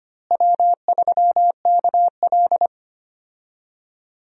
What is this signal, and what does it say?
Morse code, heard as a keyed tone, about 25 wpm, 700 Hz. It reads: W3KL
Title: USA Call Signs - 4 Characters - Repeated 3x - 25wpm